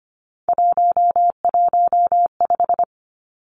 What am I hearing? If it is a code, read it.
Morse code, heard as a keyed tone, about 25 wpm, 700 Hz. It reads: 115
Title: Three Challenging Letter-Number Combinations 25wpm